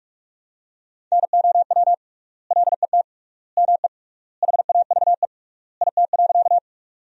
Morse code, heard as a keyed tone, about 46 wpm, 700 Hz. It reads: NOW LET ME HAVE IT.
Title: 3x Word-Spacing - Sentences from Top 300 Words - 45wpm